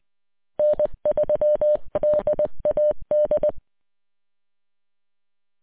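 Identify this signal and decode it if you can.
Morse code, heard as a keyed tone, about 20 wpm, 600 Hz. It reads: N3DAD
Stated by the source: Recording from W8UM on 145.230.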